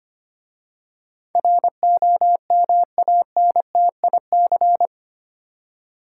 Morse code, heard as a keyed tone, about 25 wpm, 700 Hz. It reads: ROMANTIC
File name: 8 Letter Words - First 3 Letters Given - 25wpm